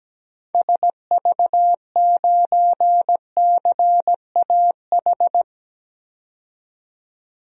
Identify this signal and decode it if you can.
Morse code, heard as a keyed tone, about 17 wpm, 700 Hz. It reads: SV9CAH
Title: DXCC-100 Call Signs - 6 Characters - 17wpm